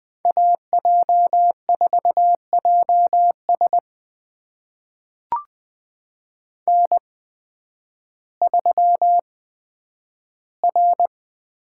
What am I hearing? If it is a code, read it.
Morse code, heard as a keyed tone, about 20 wpm, spaced as for 11 wpm, 700 Hz. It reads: AJ4JS N3R
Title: USA Call Signs - 5 Characters - 8x Character Spacing - 20wpm